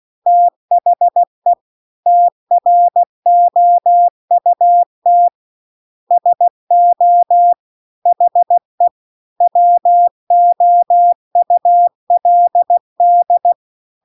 Morse code, heard as a keyed tone, about 16 wpm, 700 Hz. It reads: THE TROUT SO HE WOULD